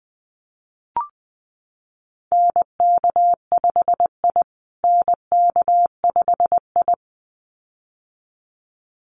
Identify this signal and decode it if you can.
Morse code, heard as a keyed tone, about 20 wpm, 700 Hz. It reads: NK5I NK5I
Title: USA Call Signs Repeated 2x 20wpm